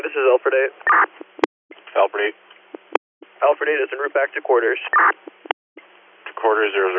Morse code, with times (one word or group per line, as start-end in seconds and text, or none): none